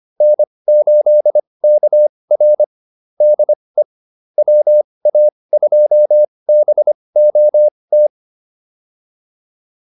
Morse code, heard as a keyed tone, about 25 wpm, spaced as for 20 wpm, 600 Hz. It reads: N8KR DE WA2BOT